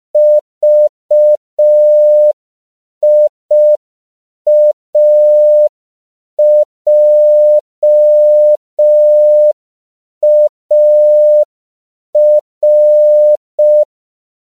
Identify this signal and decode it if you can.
Morse code, heard as a keyed tone, about 5 wpm, 600 Hz. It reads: VIAJAR
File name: CW / 5PPM